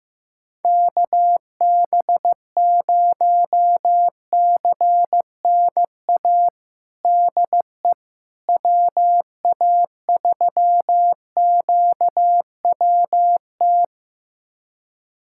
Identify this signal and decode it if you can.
Morse code, heard as a keyed tone, about 15 wpm, 700 Hz. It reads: KB0CNA DE WA3QWT
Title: USA Callsign DE Callsign - 6 Characters - 15wpm